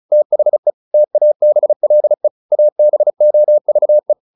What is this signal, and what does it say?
Morse code, heard as a keyed tone, about 34 wpm, 600 Hz. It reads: THE TABLE ABOVE